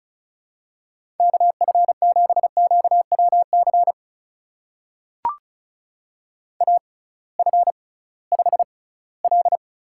Morse code, heard as a keyed tone, about 36 wpm, spaced as for 21 wpm, 700 Hz. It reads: KF7QWC AF5L
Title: USA Call Signs - 6x Character Spacing - 35wpm